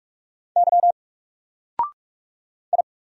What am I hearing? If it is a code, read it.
Morse code, heard as a keyed tone, about 44 wpm, 700 Hz. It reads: Y I
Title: Single Character (Letter:Number:Punctuation:Prosign) 45wpm